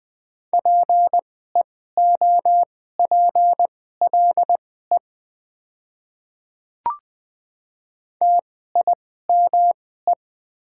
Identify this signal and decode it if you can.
Morse code, heard as a keyed tone, about 20 wpm, spaced as for 14 wpm, 700 Hz. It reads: PEOPLE TIME
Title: 2x Character Spacing - Top 200 Words - 20wpm